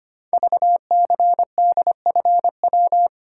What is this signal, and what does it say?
Morse code, heard as a keyed tone, about 25 wpm, 700 Hz. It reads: VCDFW
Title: Code Groups - 5 Letters - 25wpm